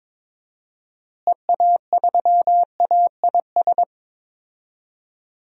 Morse code, heard as a keyed tone, about 22 wpm, 700 Hz. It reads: EA3AIS
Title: DXCC-100 Call Signs - 6 Characters - Repeated 2x - 22wpm